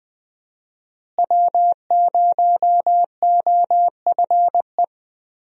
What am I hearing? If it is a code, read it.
Morse code, heard as a keyed tone, about 20 wpm, 700 Hz. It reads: W0OFE
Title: USA Call Signs - 5 Characters - Repeated 3x - 20wpm